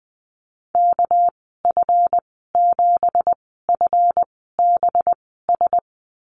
Morse code, heard as a keyed tone, about 20 wpm, spaced as for 14 wpm, 700 Hz. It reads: KF7FBS
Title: 219 Lesson 35g - Number 0 - Call Signs 20wpm